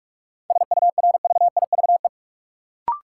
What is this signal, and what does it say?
Morse code, heard as a keyed tone, about 43 wpm, 700 Hz. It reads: SURVIVE
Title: Sentences from Top 1300 Words - New Word Drill - 45wpm